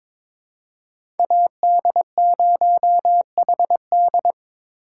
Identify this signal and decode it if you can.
Morse code, heard as a keyed tone, about 21 wpm, 700 Hz. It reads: AD0HD
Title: USA Call Signs - 5 Characters - 2x Character Spacing - 22wpm